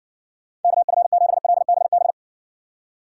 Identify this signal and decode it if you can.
Morse code, heard as a keyed tone, about 68 wpm, 700 Hz. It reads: KF7LBB